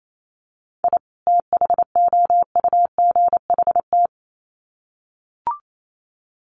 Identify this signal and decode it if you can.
Morse code, heard as a keyed tone, about 28 wpm, 700 Hz. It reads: I THOUGHT
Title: Sets of 2 Words 28wpm